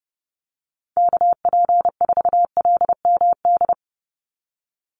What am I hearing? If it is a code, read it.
Morse code, heard as a keyed tone, about 30 wpm, 700 Hz. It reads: KP4LMD
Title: USA Call Signs 30wpm